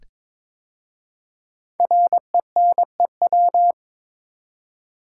Morse code, heard as a keyed tone, about 22 wpm, 690 Hz. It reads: RENEW
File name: 5 Letter Words - First 3 Letters Given - 22wpm